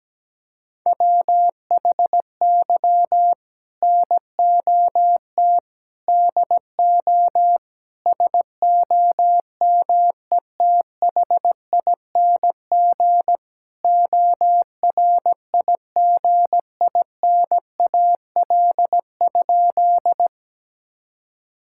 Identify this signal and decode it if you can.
Morse code, heard as a keyed tone, about 17 wpm, 700 Hz. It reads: WHY NOT DO SOMETHING ORIGINAL?
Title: Sentences from Top 1000 Words - Review Part 6 - 17wpm